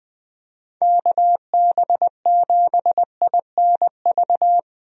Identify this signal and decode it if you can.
Morse code, heard as a keyed tone, about 20 wpm, 700 Hz. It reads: KB7INV
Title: USA Call Signs - 6 Characters - Repeated 2x - 20wpm